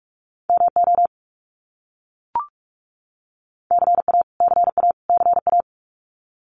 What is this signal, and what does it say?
Morse code, heard as a keyed tone, about 44 wpm, 700 Hz. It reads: MO CA CA CA
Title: US States Repeated 3x 45wpm